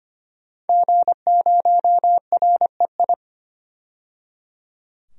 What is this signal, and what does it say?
Morse code, heard as a keyed tone, about 25 wpm, 700 Hz. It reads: G0REI
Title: DXCC-100 Call Signs - 5 Characters - 25wpm